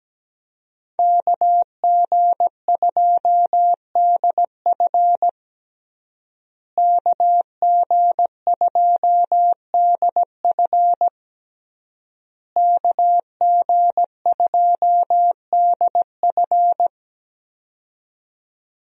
Morse code, heard as a keyed tone, about 17 wpm, 700 Hz. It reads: KG2DF KG2DF KG2DF
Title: USA Call Signs - 5 Characters - Repeated 3x - 17wpm